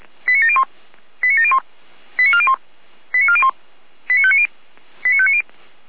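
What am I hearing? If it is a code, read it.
Five-tone selective call (ZVEI), heard as quick runs of five tones, tones of about 70 ms. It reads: 89701 89701 89401 89401 89590 89590
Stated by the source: ZVEI five-tone call received with an SDR.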